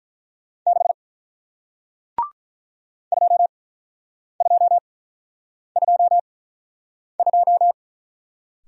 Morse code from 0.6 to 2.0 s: 6

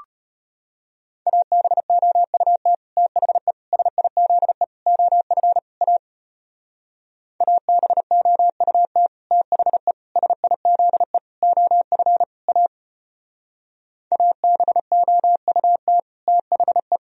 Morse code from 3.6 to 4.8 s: SIZE

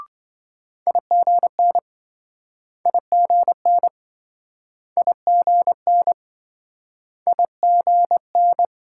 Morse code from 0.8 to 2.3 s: IGN

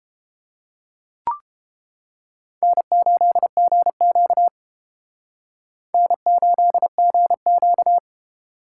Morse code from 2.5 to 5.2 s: N8GQ

